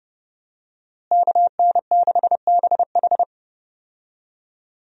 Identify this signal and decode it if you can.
Morse code, heard as a keyed tone, about 30 wpm, 700 Hz. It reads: KN6BH
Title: USA Call Signs - 5 Characters - Repeated 2x - 30wpm